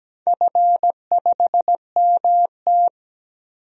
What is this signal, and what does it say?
Morse code, heard as a keyed tone, about 17 wpm, 700 Hz. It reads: F5MT
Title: DXCC-100 Call Signs - 4 Characters - 17wpm